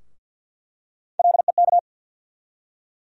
Morse code, heard as a keyed tone, about 46 wpm, 700 Hz. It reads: LEK
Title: USA Call Sign Postfixes - 3 Letters 50wpm